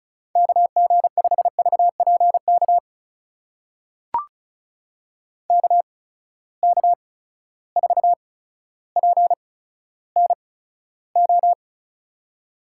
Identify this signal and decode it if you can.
Morse code, heard as a keyed tone, about 36 wpm, spaced as for 15 wpm, 700 Hz. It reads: KG5VPK KK4PNO
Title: USA Call Signs - 6 Characters - 8x Character Spacing - 35wpm